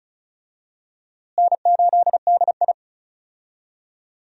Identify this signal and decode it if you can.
Morse code, heard as a keyed tone, about 34 wpm, 700 Hz. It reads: N8DI
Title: USA Call Signs - 4 Characters - Repeated 3x - 35wpm